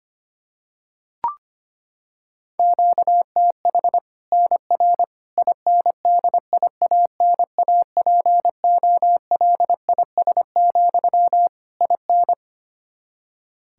Morse code, heard as a keyed tone, about 25 wpm, 700 Hz. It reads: QTH NR INDIANAPOLIS, IN
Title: Practice QSO - Exchange 1 - Beginning - 25wpm